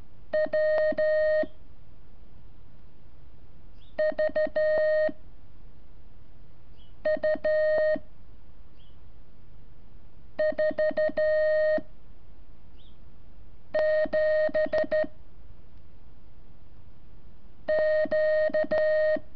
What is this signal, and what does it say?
Morse code, heard as a keyed tone, about 13 wpm, 620 Hz. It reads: W V U 4 7 Q
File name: curso / ejercicio 27